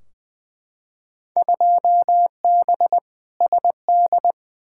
Morse code, heard as a keyed tone, about 21 wpm, 700 Hz. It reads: 2B SD
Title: Field Day Exchanges Encore - 20wpm